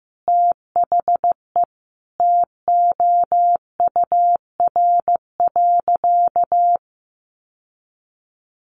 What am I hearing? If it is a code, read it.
Morse code, heard as a keyed tone, about 15 wpm, 700 Hz. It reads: THE TOUR.
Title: Sentences from Top 1400 Words 15wpm